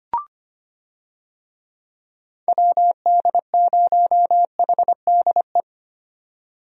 Morse code, heard as a keyed tone, about 25 wpm, 700 Hz. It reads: WD0HDE
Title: USA Call Signs - 6 Characters - 25wpm